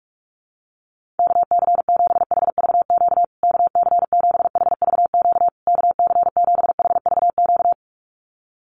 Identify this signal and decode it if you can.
Morse code, heard as a keyed tone, about 44 wpm, 700 Hz. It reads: KC7HVQ KC7HVQ KC7HVQ
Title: USA Call Signs Repeated 3x 45wpm